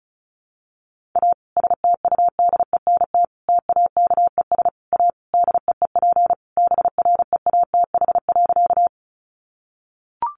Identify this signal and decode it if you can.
Morse code, heard as a keyed tone, about 34 wpm, 700 Hz. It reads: A STUDENT TAKES A DEEP BREATH.
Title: Additional Easy Phrases 35wpm